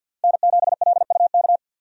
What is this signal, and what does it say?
Morse code, heard as a keyed tone, about 45 wpm, 700 Hz. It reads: N7LUK
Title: USA Call Signs - 5 Characters - Repeated 3x - 50wpm